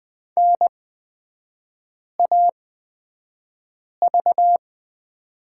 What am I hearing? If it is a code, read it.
Morse code, heard as a keyed tone, about 20 wpm, spaced as for 5 wpm, 690 Hz. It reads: NAV
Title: 236 Lesson 38e - Letter q - Three Characters 20wpm